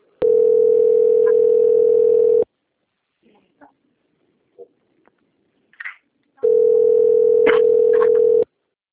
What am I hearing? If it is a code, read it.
Ringback tone (North America), 440 + 480 Hz.
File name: rg-1600-unknown- - - .18016